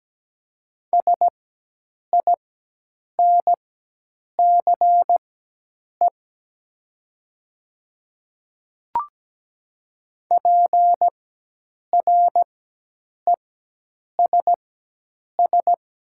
Morse code, heard as a keyed tone, about 17 wpm, spaced as for 8 wpm, 700 Hz. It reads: SINCE PRESS
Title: 4x Character Spacing - Top 300 Words - 17wpm